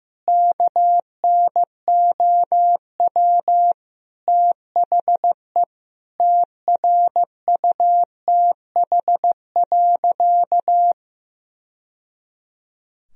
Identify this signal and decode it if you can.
Morse code, heard as a keyed tone, about 16 wpm, 700 Hz. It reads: KNOW THE TRUTH.
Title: Sentences from Top 1000 Words - Review Part 3 - 15wpm